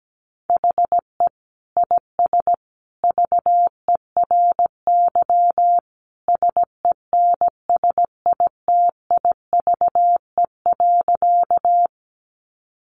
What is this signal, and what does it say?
Morse code, heard as a keyed tone, about 17 wpm, 700 Hz. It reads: HE IS VERY SENSITIVE.
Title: Sentences from Top 1400 Words 17wpm